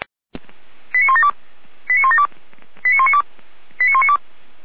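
Five-tone selective call (ZVEI), heard as quick runs of five tones, tones of about 70 ms. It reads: 89172 89172 89192 89192
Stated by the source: ZVEI five-tone call received by SDR.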